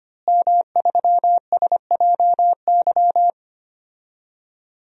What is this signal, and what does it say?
Morse code, heard as a keyed tone, about 25 wpm, 700 Hz. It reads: M3SJY
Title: DXCC-100 Call Signs - 5 Characters - Repeated 3x - 25wpm